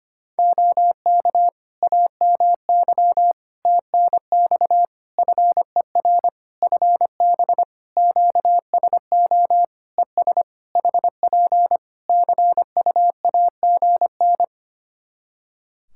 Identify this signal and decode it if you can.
Morse code, heard as a keyed tone, about 25 wpm, 700 Hz. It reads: OK AMY TNX FER FB QSO ES HP CUAGN
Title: Practice QSO - Exchange 4 - Ending - 25wpm